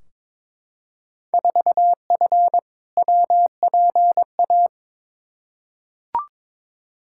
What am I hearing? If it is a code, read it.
Morse code, heard as a keyed tone, about 21 wpm, 700 Hz. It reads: 4F WPA
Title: Field Day Exchanges - 22wpm